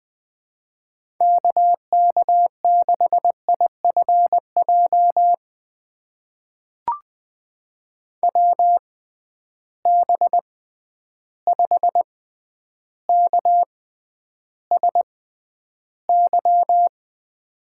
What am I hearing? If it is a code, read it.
Morse code, heard as a keyed tone, about 20 wpm, spaced as for 10 wpm, 700 Hz. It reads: KK6IFJ WB5KSY
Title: USA Call Signs - 6 Characters - 6x Character Spacing - 20wpm